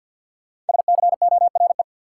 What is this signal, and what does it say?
Morse code, heard as a keyed tone, about 52 wpm, 700 Hz. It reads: SCORE
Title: Sentences from Top 1600 Words - New Word Drill - 50wpm